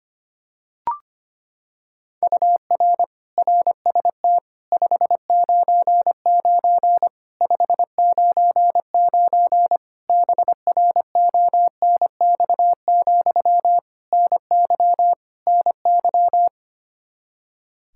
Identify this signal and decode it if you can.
Morse code, heard as a keyed tone, about 25 wpm, 700 Hz. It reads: UR RST 599 599 BRONX, NY NY
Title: Practice QSO - Exchange 1 - Beginning - 25wpm